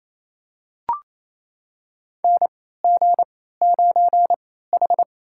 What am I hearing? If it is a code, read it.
Morse code, heard as a keyed tone, about 28 wpm, spaced as for 16 wpm, 700 Hz. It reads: NG9H
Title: USA Call Signs - 3x Character Spacing - 28wpm